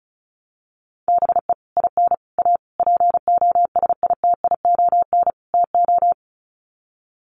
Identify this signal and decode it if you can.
Morse code, heard as a keyed tone, about 34 wpm, 700 Hz. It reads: BE IN A POSITION TO
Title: Sets of 5 Words Encore2 35wpm